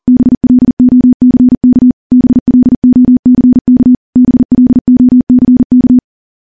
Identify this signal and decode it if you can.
Morse code, heard as a keyed tone, about 43 wpm, 250 Hz. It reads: BLOCK BLOCK BLOCK
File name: morse / morse-training-40WPM